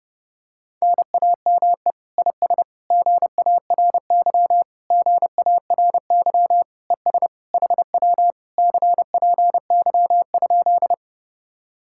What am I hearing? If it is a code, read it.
Morse code, heard as a keyed tone, about 28 wpm, 700 Hz. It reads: NAME IS GARY GARY ES HW CPY?
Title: Speed-Racing Common QSO Phrases Encore - 30wpm